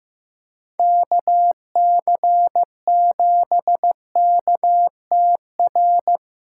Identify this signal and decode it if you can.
Morse code, heard as a keyed tone, about 16 wpm, 700 Hz. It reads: KC7KTR
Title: USA Callsign DE Callsign 15wpm